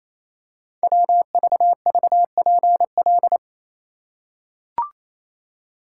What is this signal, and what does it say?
Morse code, heard as a keyed tone, about 29 wpm, 690 Hz. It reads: WVVPL
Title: Code Groups - 5 Letter:Numbers - 28wpm